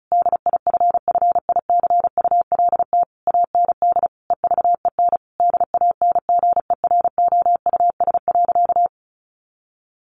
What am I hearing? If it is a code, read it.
Morse code, heard as a keyed tone, about 34 wpm, 700 Hz. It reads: DIFFICULT AND EVEN DANGEROUS.
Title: Sentences from Top 1200 Words 35wpm